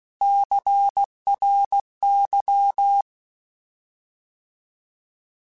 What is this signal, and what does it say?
Morse code, heard as a keyed tone, about 16 wpm, 800 Hz. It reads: CRY
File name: Alphabetical / Alphabetical mp3s / Cs @15